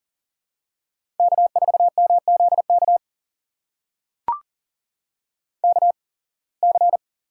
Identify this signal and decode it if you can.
Morse code, heard as a keyed tone, about 39 wpm, spaced as for 26 wpm, 700 Hz. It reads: K4MZK KC
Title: USA Call Signs - 8x Character Spacing - 40wpm